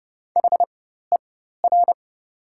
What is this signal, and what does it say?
Morse code, heard as a keyed tone, about 31 wpm, spaced as for 14 wpm, 700 Hz. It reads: HER
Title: 4x Character Spacing - Top 200 Words - 30wpm